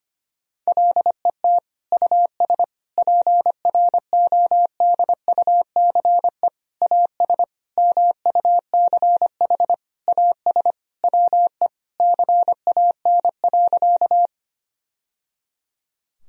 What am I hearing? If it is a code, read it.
Morse code, heard as a keyed tone, about 25 wpm, 700 Hz. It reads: LET US PRODUCE AS MUCH AS WE CAN.
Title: Sentences from Top 1000 Words - Review Part 3 - 25wpm